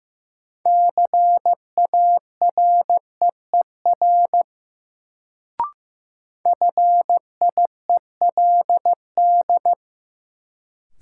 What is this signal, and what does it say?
Morse code, heard as a keyed tone, about 15 wpm, 700 Hz. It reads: CAREER FIELD
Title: Sentences from Top 1000 Words - New Word Drill - 15wpm